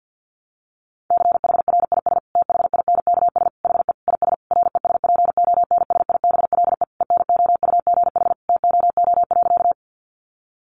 Morse code, heard as a keyed tone, about 48 wpm, 700 Hz. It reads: CHRIS THINKS HE IS RESPONSIBLE ENOUGH TOO.
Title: Easy Phrases 50wpm